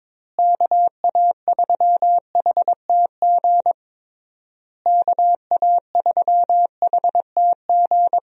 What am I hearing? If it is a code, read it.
Morse code, heard as a keyed tone, about 22 wpm, 700 Hz. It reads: KA3HTG KA3HTG
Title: USA Call Signs - 6 Characters - Repeated 3x - 22wpm